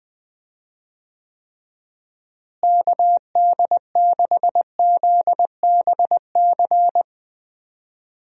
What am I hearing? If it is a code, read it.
Morse code, heard as a keyed tone, about 20 wpm, 700 Hz. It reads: KD6ZBC
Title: USA Call Signs - 6 Characters - Repeated 3x - 20wpm